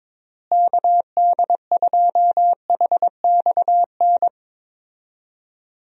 Morse code, heard as a keyed tone, about 22 wpm, 700 Hz. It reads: KD2HXN
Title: USA Call Signs - 6 Characters - Repeated 3x - 22wpm